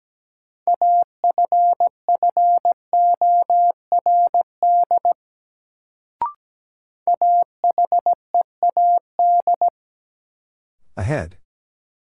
Morse code, heard as a keyed tone, about 17 wpm, 700 Hz. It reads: AFFORD AHEAD
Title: Sentences from Top 1100 Words - New Word Drill Alt - 17wpm